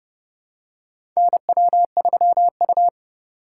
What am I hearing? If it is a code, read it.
Morse code, heard as a keyed tone, about 30 wpm, 700 Hz. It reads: NW3U
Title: USA Call Signs - 4 Characters - 3x Character Spacing - 30wpm